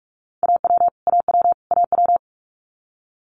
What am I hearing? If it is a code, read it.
Morse code, heard as a keyed tone, about 46 wpm, 690 Hz. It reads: AW AW AW
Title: Multi-letter Phonemes Repeated 3x 45wpm